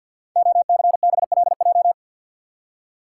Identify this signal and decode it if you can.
Morse code, heard as a keyed tone, about 52 wpm, 700 Hz. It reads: OCBLJ
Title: Code Groups - 5 Letters - 50wpm